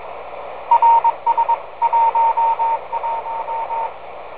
Morse code, heard as a keyed tone, about 22 wpm, 920 Hz, with radio noise. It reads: RS 11
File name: rs-11 cw telemetry 10m downlink